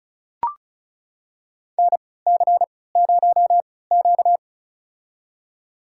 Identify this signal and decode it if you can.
Morse code, heard as a keyed tone, about 34 wpm, spaced as for 20 wpm, 700 Hz. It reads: NC0Q
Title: USA Call Signs - 4 Characters - 3x Character Spacing - 35wpm